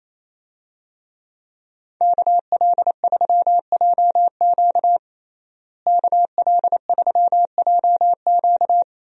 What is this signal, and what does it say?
Morse code, heard as a keyed tone, about 28 wpm, 700 Hz. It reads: KL3JQ KL3JQ
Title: USA Call Signs - 5 Characters - Repeated 2x - 28wpm